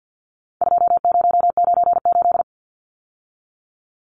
Morse code, heard as a keyed tone, about 50 wpm, 700 Hz. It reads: 2098
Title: Numbers - 4-Digit 50wpm